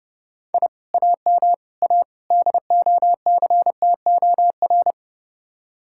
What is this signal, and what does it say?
Morse code, heard as a keyed tone, about 30 wpm, 700 Hz. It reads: I AM A DOCTOR